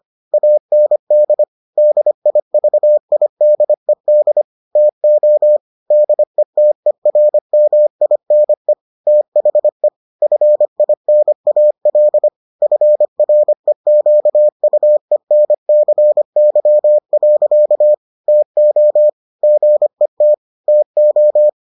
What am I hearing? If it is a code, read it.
Morse code, heard as a keyed tone, about 25 wpm, 600 Hz. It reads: AND DIVIDED TO DETERMINE THE FINAL FREQUENCY. TO GET TO